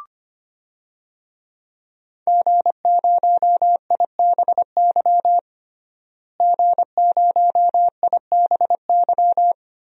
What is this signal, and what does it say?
Morse code, heard as a keyed tone, about 25 wpm, 700 Hz. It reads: G0IBY G0IBY
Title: DXCC-100 Call Signs - 5 Characters - Repeated 2x - 25wpm